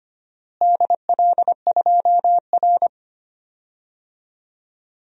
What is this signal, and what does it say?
Morse code, heard as a keyed tone, about 25 wpm, 700 Hz. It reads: DL2R
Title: DXCC-100 Call Signs - 4 Characters - Repeated 3x - 25wpm